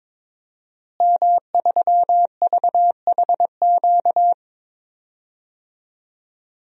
Morse code, heard as a keyed tone, about 22 wpm, 700 Hz. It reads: M3VHQ
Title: DXCC-100 Call Signs - 5 Characters - Repeated 2x - 22wpm